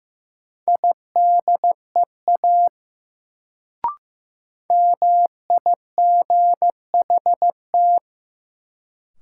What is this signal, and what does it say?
Morse code, heard as a keyed tone, about 15 wpm, 700 Hz. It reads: IDEA MIGHT
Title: Sentences from Top 300 Words - New Word Drill - 15wpm